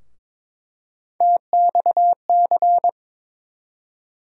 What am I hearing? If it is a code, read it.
Morse code, heard as a keyed tone, about 22 wpm, 700 Hz. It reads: TXC
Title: USA Call Sign Postfixes - 3 Letters 22wpm